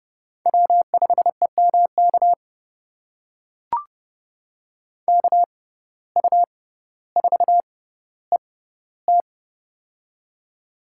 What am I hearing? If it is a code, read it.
Morse code, heard as a keyed tone, about 30 wpm, spaced as for 15 wpm, 700 Hz. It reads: W5EMK KU4ET
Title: USA Call Signs - 5 Characters - 6x Character Spacing - 30wpm